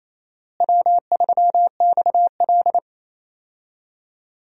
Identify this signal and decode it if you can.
Morse code, heard as a keyed tone, about 28 wpm, 700 Hz. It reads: W3XL
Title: USA Call Signs - 4 Characters - Repeated 3x - 28wpm